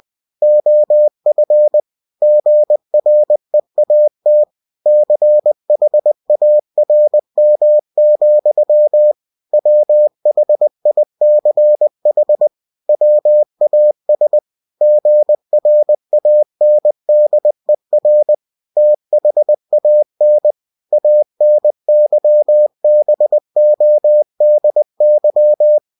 Morse code, heard as a keyed tone, about 20 wpm, 600 Hz. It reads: OF GREAT CHARM, WHICH WAS GRANDER THAN ANYBODY
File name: cw-books / winnie-the-pooh / winnie-20wpm-page-0028